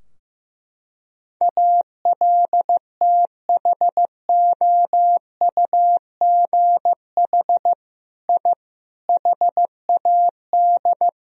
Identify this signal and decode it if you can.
Morse code, heard as a keyed tone, about 15 wpm, 700 Hz. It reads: ALTHOUGH I HAD
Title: Sentences from Top 1000 Words - Review Part 3 - 15wpm